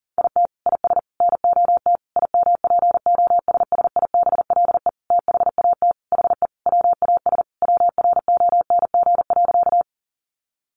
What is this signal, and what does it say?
Morse code, heard as a keyed tone, about 40 wpm, 700 Hz. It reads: IT IS NOT IMPOSSIBLE THAT HE WAS WRONG.
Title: Sentences from Top 1100 Words 40wpm